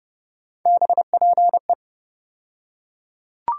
Morse code, heard as a keyed tone, about 31 wpm, 700 Hz. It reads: BPE
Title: USA Call Sign Postfixes - 3 Letters 30wpm